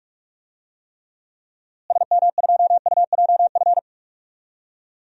Morse code, heard as a keyed tone, about 41 wpm, 700 Hz. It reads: SM2UJF
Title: DXCC-100 Call Signs - 6 Characters - Repeated 2x - 45wpm